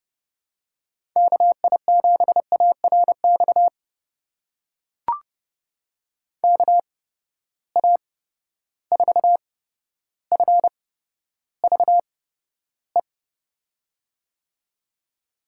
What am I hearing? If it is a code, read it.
Morse code, heard as a keyed tone, about 31 wpm, spaced as for 13 wpm, 700 Hz. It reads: KI7ARX KA4FVE
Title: USA Call Signs - 6 Characters - 8x Character Spacing - 30wpm